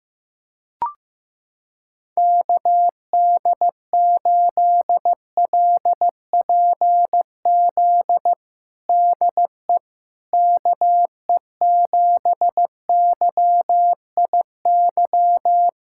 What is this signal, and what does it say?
Morse code, heard as a keyed tone, about 15 wpm, 700 Hz. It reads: KD8LPZ DE KE7YIY
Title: USA Callsign DE Callsign - 6 Characters - 15wpm